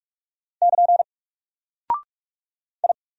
Morse code, heard as a keyed tone, about 46 wpm, 700 Hz. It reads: <KN> I
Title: Single Character (Letter:Number:Punctuation:Prosign) 45wpm